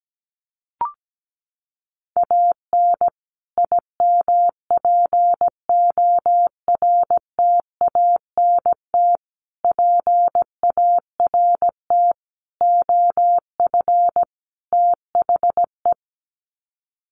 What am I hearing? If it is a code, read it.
Morse code, heard as a keyed tone, about 17 wpm, 700 Hz. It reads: AN IMPORTANT PART OF THE
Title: Sets of 5 Words Encore 17wpm